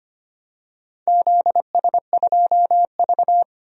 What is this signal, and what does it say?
Morse code, heard as a keyed tone, about 25 wpm, 700 Hz. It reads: ZS2V
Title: DXCC-100 Call Signs - 4 Characters - 25wpm